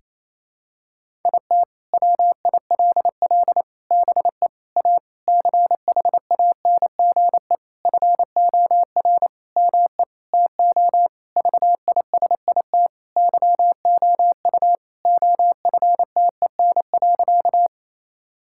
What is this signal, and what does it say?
Morse code, heard as a keyed tone, about 28 wpm, 700 Hz. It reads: IT WILL BE A CHANGE FOR ME TO VISIT YOU OFTEN.
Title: Sentences from Top 1000 Words - Review Part 6 - 28wpm